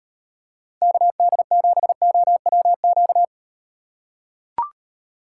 Morse code, heard as a keyed tone, about 38 wpm, 700 Hz. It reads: KD7OWQ